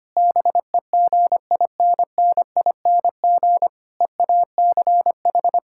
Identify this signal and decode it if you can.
Morse code, heard as a keyed tone, about 25 wpm, 700 Hz. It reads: BEGINNING EACH
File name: practice / 25wpm / Difficult Sentences with Hints - 25wpm